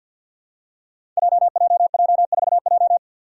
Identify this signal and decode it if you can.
Morse code, heard as a keyed tone, about 53 wpm, 700 Hz. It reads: JJJ4J
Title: Five Challenging Letter-Number Combinations 50wpm